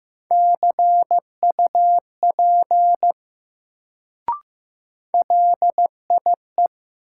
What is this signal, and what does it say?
Morse code, heard as a keyed tone, about 15 wpm, 700 Hz. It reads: CUP LIE
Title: Sentences from Top 1600 Words - New Word Drill - 15wpm